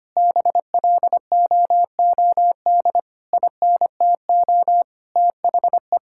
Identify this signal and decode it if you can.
Morse code, heard as a keyed tone, about 25 wpm, 700 Hz. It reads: BLOOD INTO THE
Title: Sentences from Top 1600 Words 25wpm